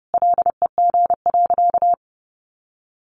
Morse code, heard as a keyed tone, about 30 wpm, 700 Hz. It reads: LEG.
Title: Additional Easy Phrases 30wpm